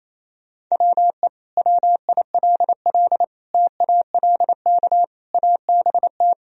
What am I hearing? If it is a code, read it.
Morse code, heard as a keyed tone, about 28 wpm, 700 Hz. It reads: WE WILL TALK ABT
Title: Sentences from Top 400 Words and Abbreviations 28wpm